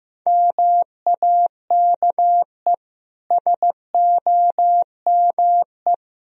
Morse code, heard as a keyed tone, about 15 wpm, 700 Hz. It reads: MAKE SOME
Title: Sentences from Top 1000 Words - Review Part 3 - 15wpm